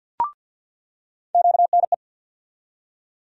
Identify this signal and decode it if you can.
Morse code, heard as a keyed tone, about 48 wpm, 700 Hz. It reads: QNE
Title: USA Call Sign Postfixes - 3 Letters 50wpm